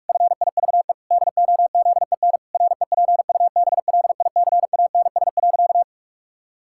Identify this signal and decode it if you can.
Morse code, heard as a keyed tone, about 44 wpm, 700 Hz. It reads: FIVE DOZEN REPUBLICANS.
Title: Pangram Challenge v2 - 45wpm